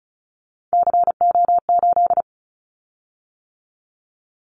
Morse code, heard as a keyed tone, about 34 wpm, 700 Hz. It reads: CO8
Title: Callsign Prefixes for Top 100 DXCC - 35wpm